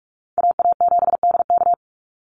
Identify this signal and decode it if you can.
Morse code, heard as a keyed tone, about 46 wpm, 700 Hz. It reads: AA7DK
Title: NA SOTA Chaser Callsigns 45wpm